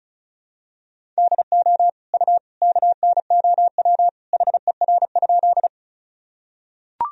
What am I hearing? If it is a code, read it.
Morse code, heard as a keyed tone, about 34 wpm, 700 Hz. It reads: DO U KNOW HER?
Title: Sentences from Top 100 Words and Abbreviations 35wpm